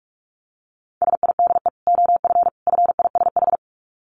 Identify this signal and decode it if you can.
Morse code, heard as a keyed tone, about 44 wpm, 700 Hz. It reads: SIDE OF FISH
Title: Dit Challenge 45wpm